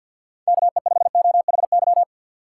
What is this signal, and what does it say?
Morse code, heard as a keyed tone, about 51 wpm, 690 Hz. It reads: KE5OHY
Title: USA Call Signs - 6 Characters - 3x Character Spacing - 50wpm